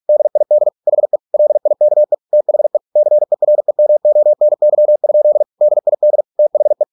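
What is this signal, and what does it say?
Morse code, heard as a keyed tone, about 48 wpm, 600 Hz. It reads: DID HE LIKE THE CEREMONY? DID THE
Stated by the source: Morse practice at 45 wpm, the novel Never by Ken Follett in English, from CW med Gnister.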